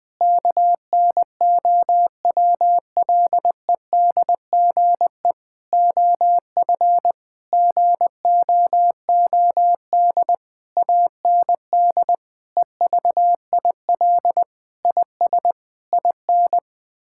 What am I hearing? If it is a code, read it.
Morse code, heard as a keyed tone, about 20 wpm, 700 Hz. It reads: KNOWLEDGE OF GOOD AND EVIL IS IN
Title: Sentences with 9-Word Phrases - 20wpm